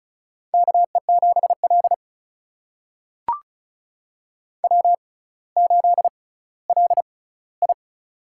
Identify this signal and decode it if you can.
Morse code, heard as a keyed tone, about 36 wpm, spaced as for 18 wpm, 700 Hz. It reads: KE7L W8LI
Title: USA Call Signs - 4 Characters - 6x Character Spacing - 35wpm